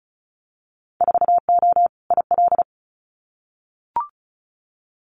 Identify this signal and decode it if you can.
Morse code, heard as a keyed tone, about 34 wpm, 700 Hz. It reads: 4O IL
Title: Winter Field Day Exchanges Encore 35wpm